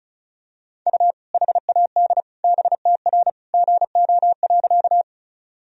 Morse code, heard as a keyed tone, about 36 wpm, 700 Hz. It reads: U HAD BTR GO.
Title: Sentences from Top 400 Words and Abbreviations 35wpm